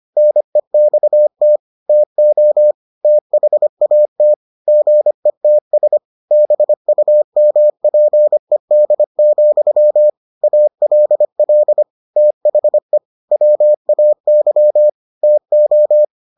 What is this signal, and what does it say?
Morse code, heard as a keyed tone, about 25 wpm, 600 Hz. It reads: NEXT TO THAT GETS BUMPED, ALL THE WAY TO